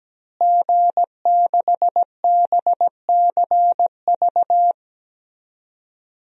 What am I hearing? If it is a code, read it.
Morse code, heard as a keyed tone, about 17 wpm, 700 Hz. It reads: G6BCV
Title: DXCC-100 Call Signs - 5 Characters - Repeated 3x - 17wpm